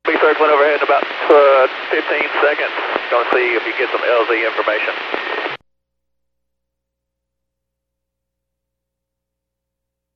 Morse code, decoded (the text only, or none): none